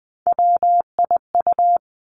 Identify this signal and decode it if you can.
Morse code, heard as a keyed tone, about 20 wpm, 700 Hz. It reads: WIU
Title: Three Challenging Letter-Number Combinations 20wpm